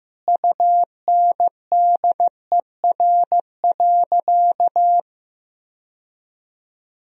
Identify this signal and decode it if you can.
Morse code, heard as a keyed tone, about 15 wpm, 700 Hz. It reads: UNDER.
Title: Sentences from Top 1000 Words - Review Part 1 - 15wpm